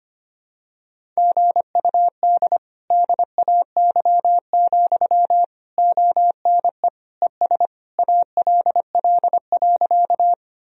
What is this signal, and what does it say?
Morse code, heard as a keyed tone, about 25 wpm, 700 Hz. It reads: GUD DAY, ONE ES ALL.
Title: Sentences from Top 200 Words and Abbreviations 25wpm